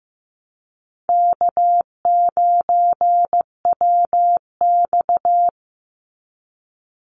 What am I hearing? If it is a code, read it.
Morse code, heard as a keyed tone, about 15 wpm, 700 Hz. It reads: K9WX
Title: Call Signs Repeated 3x - 15wpm